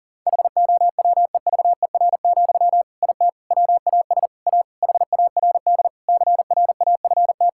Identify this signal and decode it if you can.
Morse code, heard as a keyed tone, about 37 wpm, 700 Hz. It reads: HOWEVER, IT WAS A HARD CRAFT
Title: Sentences from Top 1000 Words - Review Part 6 - 40wpm